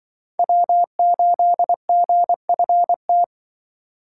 Morse code, heard as a keyed tone, about 24 wpm, 700 Hz. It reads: W8GFT
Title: Speed Racing - USA 1x3 Call Signs 20wpm